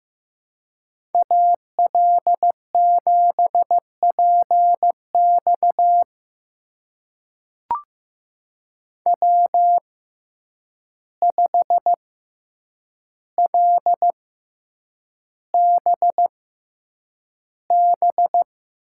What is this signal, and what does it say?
Morse code, heard as a keyed tone, about 15 wpm, spaced as for 8 wpm, 700 Hz. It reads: AL7PX W5LBB
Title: USA Call Signs - 5 Characters - 6x Character Spacing - 15wpm